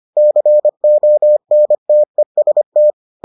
Morse code, heard as a keyed tone, about 25 wpm, 600 Hz. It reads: CONTEST